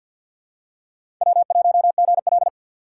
Morse code, heard as a keyed tone, about 52 wpm, 700 Hz. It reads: W1GL
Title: USA Call Signs - 4 Characters - 6x Character Spacing - 50wpm